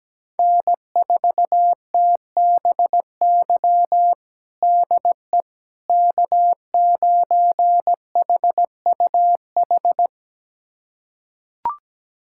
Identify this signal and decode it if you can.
Morse code, heard as a keyed tone, about 17 wpm, 700 Hz. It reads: N4TBY DE K9HUH
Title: USA Callsign DE Callsign - 5 Characters - 17wpm